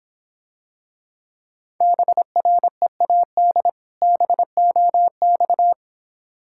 Morse code, heard as a keyed tone, about 26 wpm, 700 Hz. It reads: BREAD BOX